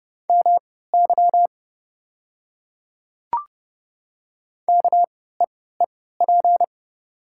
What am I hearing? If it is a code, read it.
Morse code, heard as a keyed tone, about 29 wpm, spaced as for 17 wpm, 700 Hz. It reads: MY KEEP
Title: 3x Character Spacing - Top 500 Words - 30wpm